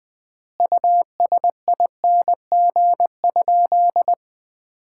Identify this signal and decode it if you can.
Morse code, heard as a keyed tone, about 20 wpm, 700 Hz. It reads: USING?
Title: Common QSO Phrases - 4x Word-Spacing - 20wpm